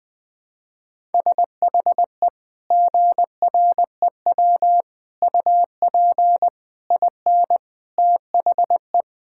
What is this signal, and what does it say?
Morse code, heard as a keyed tone, about 20 wpm, 700 Hz. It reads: SHE GREW UP IN THE
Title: Sentences with 2-Word Phrases - Spoken Beginning - 20wpm